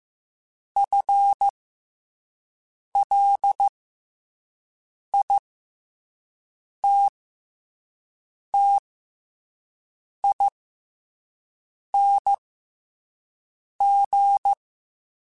Morse code, heard as a keyed tone, about 15 wpm, spaced as for 5 wpm, 800 Hz. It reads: FLITTING
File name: FL / FL mp3s / FL Prefixes 3x2 sp @5